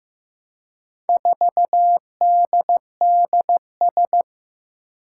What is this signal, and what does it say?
Morse code, heard as a keyed tone, about 15 wpm, 700 Hz. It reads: 4DDS
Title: Four Challenging Letter-Number Combinations 15wpm